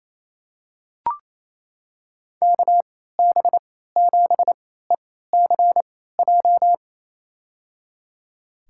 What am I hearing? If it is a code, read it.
Morse code, heard as a keyed tone, about 28 wpm, spaced as for 16 wpm, 700 Hz. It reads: KB7ECJ
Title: USA Call Signs - 6 Characters - 3x Character Spacing - 28wpm